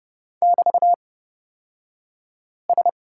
Morse code, heard as a keyed tone, about 30 wpm, spaced as for 5 wpm, 700 Hz. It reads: <BT>S
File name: Lesson 36d - BT - Two Characters 30wpm